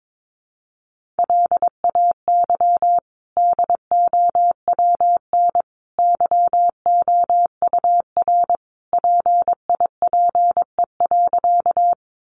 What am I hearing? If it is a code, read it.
Morse code, heard as a keyed tone, about 22 wpm, 700 Hz. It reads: LAY DOWN YOUR PIPE.
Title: Sentences from Top 1500 Words 22wpm